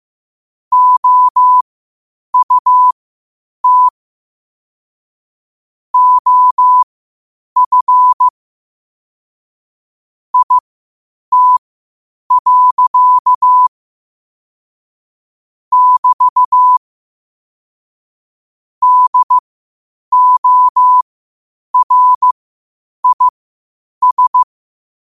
Morse code, heard as a keyed tone, about 15 wpm, spaced as for 8 wpm, 1000 Hz. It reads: OUT OF IT. <BT> DORIS
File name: qotdfiles / 8wpm